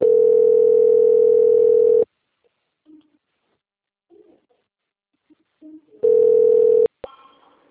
Ringback tone (North America), 440 + 480 Hz.